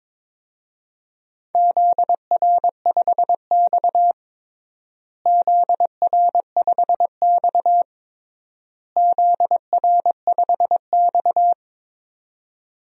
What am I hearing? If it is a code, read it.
Morse code, heard as a keyed tone, about 22 wpm, 700 Hz. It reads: ZR5X ZR5X ZR5X
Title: DXCC-100 Call Signs - 4 Characters - Repeated 3x - 22wpm